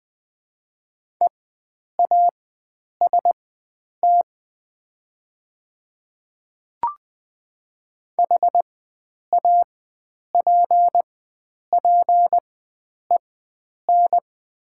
Morse code, heard as a keyed tone, about 20 wpm, spaced as for 9 wpm, 700 Hz. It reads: EAST HAPPEN
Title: 4x Character Spacing - Top 500 Words - 20wpm